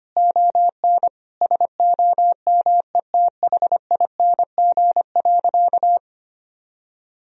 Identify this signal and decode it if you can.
Morse code, heard as a keyed tone, about 25 wpm, 700 Hz. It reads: ON SOMETHING.
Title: Sentences from Top 1000 Words - Review Part 3 - 25wpm